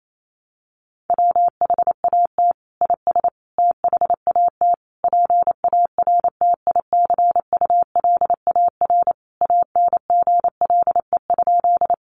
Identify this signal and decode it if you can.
Morse code, heard as a keyed tone, about 28 wpm, 700 Hz. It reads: WHAT IS THAT PARTICULAR ANGLE?
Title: Sentences from Top 1500 Words 28wpm